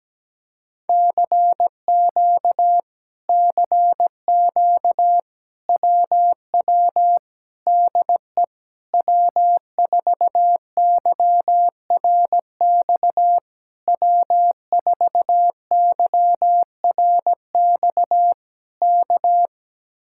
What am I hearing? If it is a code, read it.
Morse code, heard as a keyed tone, about 17 wpm, 700 Hz. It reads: CQ CQ WW DE W4YRX W4YRX K
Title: Speed-Racing Common QSO Phrases - 17wpm